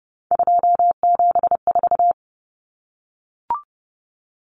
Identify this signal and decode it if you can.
Morse code, heard as a keyed tone, about 30 wpm, 700 Hz. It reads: 274
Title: Numbers - 3-Digit 30wpm